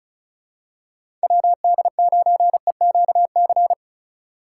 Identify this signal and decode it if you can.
Morse code, heard as a keyed tone, about 36 wpm, 700 Hz. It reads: WD9EQC
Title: USA Call Signs - 6 Characters - 4x Character Spacing - 35wpm